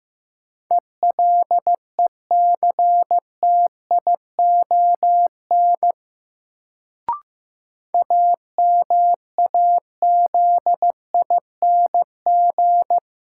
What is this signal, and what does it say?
Morse code, heard as a keyed tone, about 15 wpm, 700 Hz. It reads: ELECTION AMAZING
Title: Sentences from Top 1500 Words - New Word Drill - 15wpm